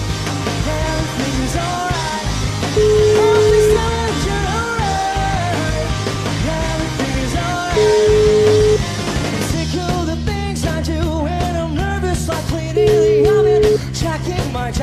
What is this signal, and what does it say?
Ringback tone (Europe, 425 Hz), with noise.